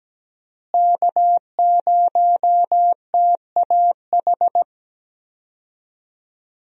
Morse code, heard as a keyed tone, about 17 wpm, 700 Hz. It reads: K0TAH
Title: USA Call Signs - 5 Characters - Repeated 3x - 17wpm